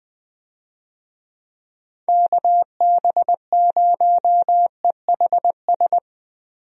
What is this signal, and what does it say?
Morse code, heard as a keyed tone, about 20 wpm, 700 Hz. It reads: KB0EHS
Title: USA Call Signs - 6 Characters - Repeated 2x - 20wpm